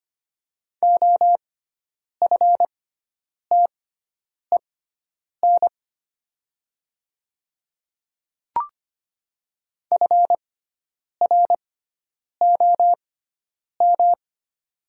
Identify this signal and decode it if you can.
Morse code, heard as a keyed tone, about 25 wpm, spaced as for 9 wpm, 700 Hz. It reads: OFTEN FROM
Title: 6x Character Spacing - Top 500 Words - 25wpm